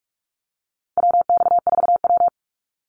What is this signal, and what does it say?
Morse code, heard as a keyed tone, about 44 wpm, 700 Hz. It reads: WX4W
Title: Call Signs Repeated 2x - 45wpm